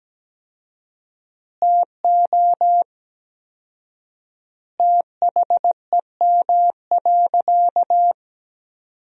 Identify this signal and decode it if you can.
Morse code, heard as a keyed tone, about 17 wpm, 700 Hz. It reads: TO THEM.
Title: 4x Word-Spacing - Sentences from Top 200 Words - 17wpm